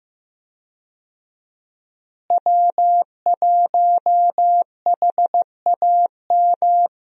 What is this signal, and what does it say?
Morse code, heard as a keyed tone, about 15 wpm, 700 Hz. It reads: W1HAM
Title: USA Call Signs - 5 Characters - Repeated 2x - 15wpm